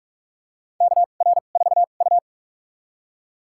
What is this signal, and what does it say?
Morse code, heard as a keyed tone, about 45 wpm, spaced as for 32 wpm, 700 Hz. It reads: KR4U
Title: USA Call Signs - 4 Characters - 2x Character Spacing - 45wpm